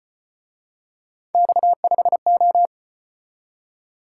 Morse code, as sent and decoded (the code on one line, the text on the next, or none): -..- ..... ---
X5O